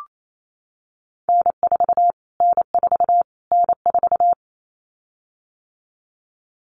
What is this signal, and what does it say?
Morse code, heard as a keyed tone, about 28 wpm, 700 Hz. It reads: N4 N4 N4
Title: Callsign Prefixes for Top 25 DXCC - Repeated 3x - 28wpm